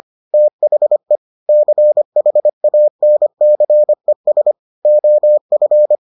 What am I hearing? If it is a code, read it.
Morse code, heard as a keyed tone, about 25 wpm, 600 Hz. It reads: THE CHANCES OF